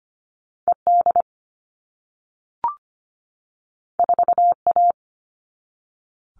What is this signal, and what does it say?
Morse code, heard as a keyed tone, about 25 wpm, 700 Hz. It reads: ED 4A
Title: All Cut Numbers - 2-Digit 25wpm